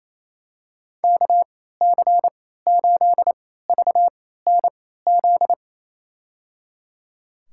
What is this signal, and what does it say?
Morse code, heard as a keyed tone, about 28 wpm, spaced as for 16 wpm, 700 Hz. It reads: KC8VNZ
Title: USA Call Signs - 6 Characters - 3x Character Spacing - 28wpm